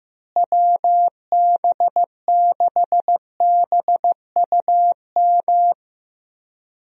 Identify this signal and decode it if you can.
Morse code, heard as a keyed tone, about 15 wpm, 700 Hz. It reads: WB6BUM
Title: USA Call Signs - 6 Characters - Repeated 2x - 15wpm